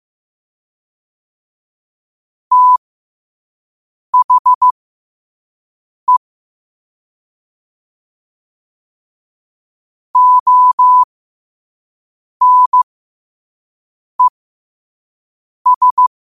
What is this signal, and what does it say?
Morse code, heard as a keyed tone, about 15 wpm, spaced as for 5 wpm, 1000 Hz. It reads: THE ONES
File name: qotdfiles / 5wpm